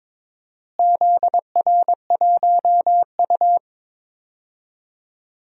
Morse code, heard as a keyed tone, about 22 wpm, 700 Hz. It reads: ZR1U
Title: DXCC-100 Call Signs - 4 Characters - Repeated 2x - 22wpm